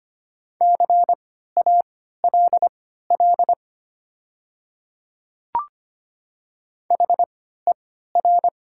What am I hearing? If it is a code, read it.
Morse code, heard as a keyed tone, about 25 wpm, spaced as for 14 wpm, 700 Hz. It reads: CALL HER
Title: 3x Character Spacing - Top 200 Words - 25wpm